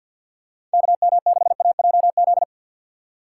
Morse code, heard as a keyed tone, about 47 wpm, 700 Hz. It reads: KM6AJZ
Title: USA Call Signs - 6 Characters - 3x Character Spacing - 50wpm